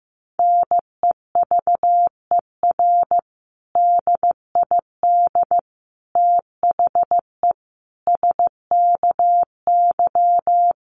Morse code, heard as a keyed tone, about 15 wpm, 700 Hz. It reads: NEVER DID THE SKY
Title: Sentences from Top 1200 Words 15wpm